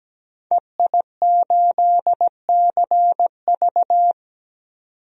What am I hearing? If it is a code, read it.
Morse code, heard as a keyed tone, about 17 wpm, 700 Hz. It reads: EI8CV
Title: DXCC-100 Call Signs - 5 Characters - 17wpm